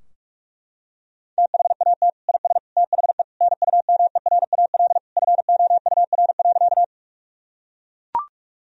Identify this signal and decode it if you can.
Morse code, heard as a keyed tone, about 42 wpm, 700 Hz. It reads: THAT IS THE NUMERAL FOUR.
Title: Sentences from Top 1000 Words - Review Part 2 - 45wpm